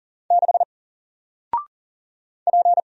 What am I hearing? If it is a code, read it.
Morse code, heard as a keyed tone, about 40 wpm, 700 Hz. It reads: 6 P
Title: Single Letter-Number 40wpm